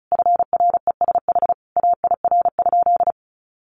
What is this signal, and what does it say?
Morse code, heard as a keyed tone, about 34 wpm, 700 Hz. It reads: FRESH AIR?
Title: Sentences from Top 1200 Words 35wpm